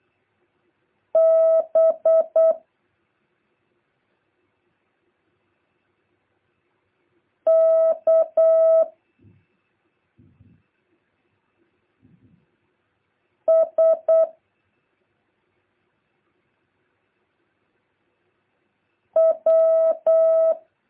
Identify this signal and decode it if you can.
Morse code, heard as a keyed tone, about 8 wpm, 650 Hz. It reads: B K S W
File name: jh8cbh / 2023cw